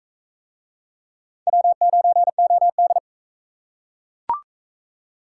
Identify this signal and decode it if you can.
Morse code, heard as a keyed tone, about 42 wpm, 700 Hz. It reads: W9OD